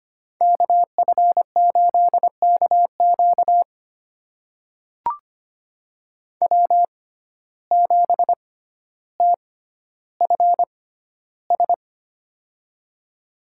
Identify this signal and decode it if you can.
Morse code, heard as a keyed tone, about 25 wpm, spaced as for 13 wpm, 700 Hz. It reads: KF8KQ W7TFS
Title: USA Call Signs - 5 Characters - 6x Character Spacing - 25wpm